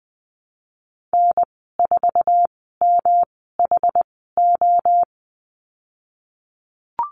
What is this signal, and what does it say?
Morse code, heard as a keyed tone, about 19 wpm, spaced as for 14 wpm, 700 Hz. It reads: N4MHO
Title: 161 Lesson 26g - Letter g - Call Signs 20wpm